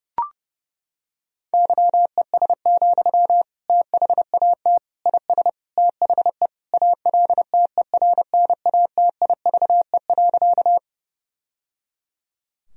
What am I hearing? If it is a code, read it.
Morse code, heard as a keyed tone, about 32 wpm, 700 Hz. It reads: YES, THAT IS THE ALTERNATIVE.
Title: Sentences from Top 1000 Words - Review Part 4 - 30wpm